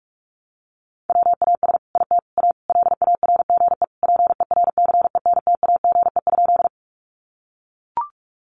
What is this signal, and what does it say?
Morse code, heard as a keyed tone, about 44 wpm, 700 Hz. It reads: WAS IT A LARGE PERCENTAGE?
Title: Sentences from Top 1300 Words 45wpm